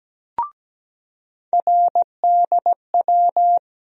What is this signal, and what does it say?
Morse code, heard as a keyed tone, about 17 wpm, 700 Hz. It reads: RDW
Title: USA Call Sign Postfixes - 3 Letters 17wpm